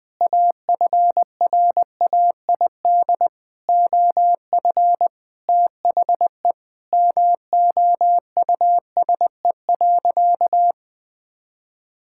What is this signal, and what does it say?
Morse code, heard as a keyed tone, about 21 wpm, 700 Hz. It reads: AFRAID OF THE MOUSE.
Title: All-Age Appropriate Jokes - 20wpm